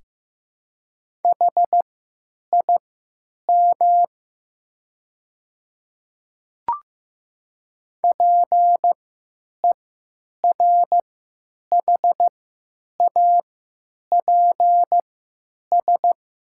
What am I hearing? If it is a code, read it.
Morse code, heard as a keyed tone, about 15 wpm, spaced as for 8 wpm, 700 Hz. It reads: HIM PERHAPS
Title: 3x Character Spacing - Top 500 Words - 15wpm